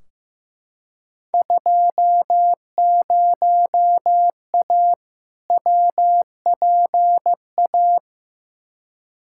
Morse code, heard as a keyed tone, about 15 wpm, 700 Hz. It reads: 20A WPA
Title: Field Day Exchanges Encore - 15wpm